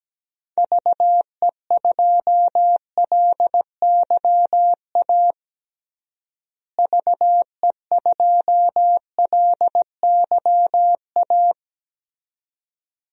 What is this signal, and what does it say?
Morse code, heard as a keyed tone, about 17 wpm, 700 Hz. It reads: VE2LYA VE2LYA
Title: DXCC-100 Call Signs - 6 Characters - Repeated 2x - 17wpm